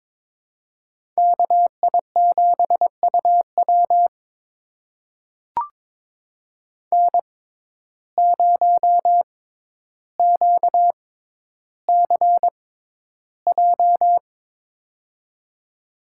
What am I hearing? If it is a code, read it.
Morse code, heard as a keyed tone, about 22 wpm, spaced as for 11 wpm, 700 Hz. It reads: KI7UW N0QCJ
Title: USA Call Signs - 5 Characters - 6x Character Spacing - 22wpm